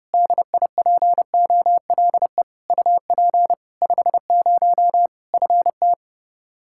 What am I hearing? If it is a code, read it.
Morse code, heard as a keyed tone, about 32 wpm, 700 Hz. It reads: DIPOLE UP 50 FT
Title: Practice QSO - Exchange 2 - Info - 30wpm